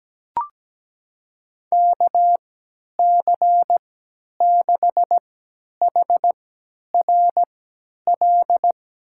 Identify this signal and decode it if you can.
Morse code, heard as a keyed tone, about 17 wpm, spaced as for 10 wpm, 700 Hz. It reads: KC6HRL
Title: USA Call Signs - 3x Character Spacing - 17wpm